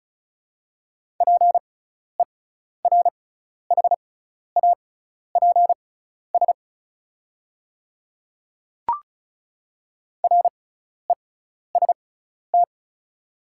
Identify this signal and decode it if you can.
Morse code, heard as a keyed tone, about 36 wpm, spaced as for 12 wpm, 690 Hz. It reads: PERHAPS REST
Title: 6x Character Spacing - Top 500 Words - 35wpm